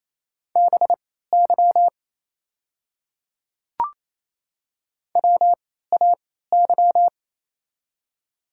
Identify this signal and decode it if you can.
Morse code, heard as a keyed tone, about 27 wpm, spaced as for 16 wpm, 700 Hz. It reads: BY WAY
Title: 3x Character Spacing - Top 200 Words - 28wpm